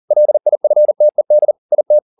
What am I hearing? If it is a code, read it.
Morse code, heard as a keyed tone, about 40 wpm, 600 Hz. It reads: LIFTED IT
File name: the old man and the sea 2 40wpm